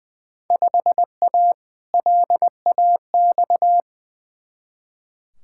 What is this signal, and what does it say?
Morse code, heard as a keyed tone, about 21 wpm, 700 Hz. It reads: 5A LAX
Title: Field Day Exchanges - 20wpm